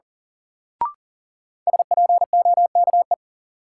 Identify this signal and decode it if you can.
Morse code, heard as a keyed tone, about 37 wpm, 700 Hz. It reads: SPOKE
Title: Sentences from Top 1200 Words - New Word Drill Alt - 40wpm